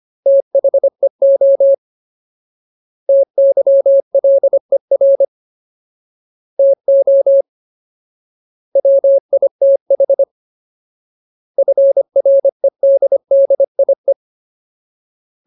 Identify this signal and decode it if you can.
Morse code, heard as a keyed tone, about 25 wpm, 550 Hz. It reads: THEO TYLER TO WITH FREDDIE
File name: Ultimate Word List / 25WPM Ultimate